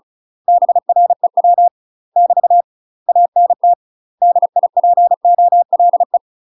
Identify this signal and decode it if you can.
Morse code, heard as a keyed tone, about 34 wpm, 700 Hz. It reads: BREW <BT> ANT DIPOLE
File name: UK MASTER QSOs TRAINING / 35WPM / UK MASTER QSOs 0004